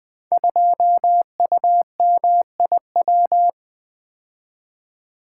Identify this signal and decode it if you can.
Morse code, heard as a keyed tone, about 20 wpm, 700 Hz. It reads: 2UMIW
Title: Code Groups - 5 Letter:Numbers - 20wpm